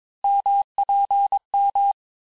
Morse code, heard as a keyed tone, about 22 wpm, 800 Hz. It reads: MPM
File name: cours-du-lundi files / cours-1587-texte-ponctuations-vitesse-16mpm-puis-21mpm